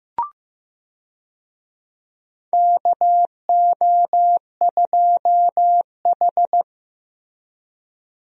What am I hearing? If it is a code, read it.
Morse code, heard as a keyed tone, about 15 wpm, 700 Hz. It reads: KO2H
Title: USA Call Signs - 4 Characters - Repeated 3x - 15wpm